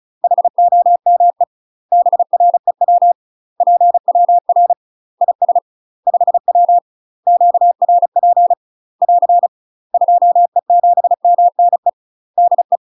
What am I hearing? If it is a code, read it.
Morse code, heard as a keyed tone, about 34 wpm, 700 Hz. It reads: HOME BREW PWR IS 5W QRP <AR> 2E7MNE DE
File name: UK MASTER QSOs TRAINING / 35WPM / UK MASTER QSOs 0009